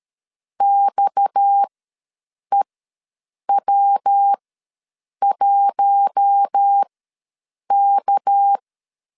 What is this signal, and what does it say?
Morse code, heard as a keyed tone, about 13 wpm, spaced as for 7 wpm, 800 Hz. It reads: XEW1K
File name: 7 GF9Z ws 12wpm